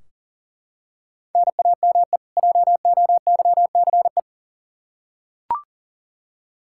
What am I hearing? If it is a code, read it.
Morse code, heard as a keyed tone, about 40 wpm, 700 Hz. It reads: NAME JOYCE
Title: K1USN SST QSO Practice 40wpm